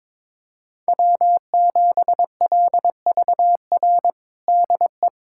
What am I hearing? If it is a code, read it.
Morse code, heard as a keyed tone, about 21 wpm, 700 Hz. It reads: W7LVR DE
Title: USA Callsign DE Callsign - 5 Characters - 22wpm